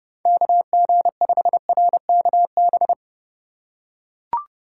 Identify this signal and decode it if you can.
Morse code, heard as a keyed tone, about 31 wpm, 700 Hz. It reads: KG5RKB
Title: USA Call Signs - 6 Characters - 2x Character Spacing - 30wpm